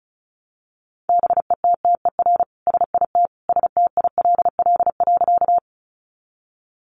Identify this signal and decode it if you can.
Morse code, heard as a keyed tone, about 34 wpm, 700 Hz. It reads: BETTER SIT STILL.
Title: Sentences from Top 500 Words 35wpm